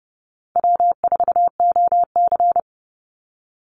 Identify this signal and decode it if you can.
Morse code, heard as a keyed tone, about 30 wpm, 700 Hz. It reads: W4OC
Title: Easy Call Signs Repeated 3x - 30wpm